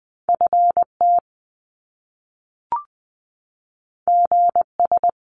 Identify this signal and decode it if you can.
Morse code, heard as a keyed tone, about 20 wpm, 700 Hz. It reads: F T G S
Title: Top Two Letter Combinations in Words 20wpm